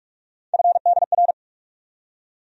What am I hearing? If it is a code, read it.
Morse code, heard as a keyed tone, about 46 wpm, 700 Hz. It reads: FDR
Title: USA Call Sign Postfixes - 3 Letters 45wpm